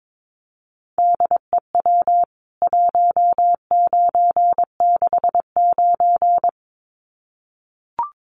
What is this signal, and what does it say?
Morse code, heard as a keyed tone, about 22 wpm, 700 Hz. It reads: DEW 1969
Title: CWT Exchange 22wpm